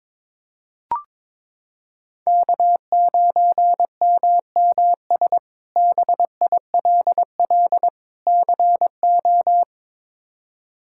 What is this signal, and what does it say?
Morse code, heard as a keyed tone, about 21 wpm, 700 Hz. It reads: K9MMS BILL CO
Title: K1USN SST QSO Practice - 22wpm